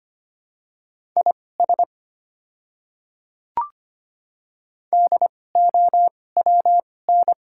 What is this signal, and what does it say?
Morse code, heard as a keyed tone, about 25 wpm, spaced as for 18 wpm, 690 Hz. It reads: IS DOWN
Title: 2x Character Spacing - Top 300 Words - 25wpm